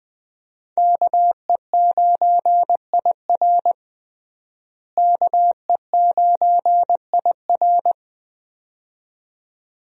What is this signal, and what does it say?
Morse code, heard as a keyed tone, about 21 wpm, 700 Hz. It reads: KE9IR KE9IR
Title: USA Call Signs - 5 Characters - Repeated 2x - 20wpm